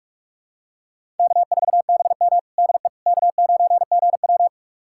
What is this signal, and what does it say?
Morse code, heard as a keyed tone, about 46 wpm, 700 Hz. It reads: K4BM DE K9GW
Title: USA Callsign DE Callsign - 4 Characters - 45wpm